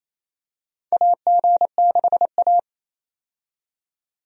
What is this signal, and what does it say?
Morse code, heard as a keyed tone, about 28 wpm, 700 Hz. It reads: AG6A
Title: USA Call Signs - 4 Characters - 28wpm